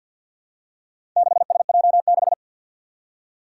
Morse code, heard as a keyed tone, about 53 wpm, 700 Hz. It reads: 6SJ6
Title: Four Challenging Letter-Number Combinations 50wpm